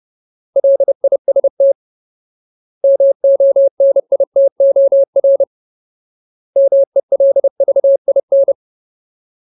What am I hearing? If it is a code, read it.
Morse code, heard as a keyed tone, about 30 wpm, 550 Hz. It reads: LIST MONITOR MELVIN
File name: Ultimate Word List / 30WPM Ultimate